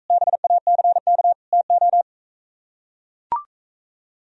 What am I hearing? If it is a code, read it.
Morse code, heard as a keyed tone, about 43 wpm, 700 Hz. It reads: BACK TO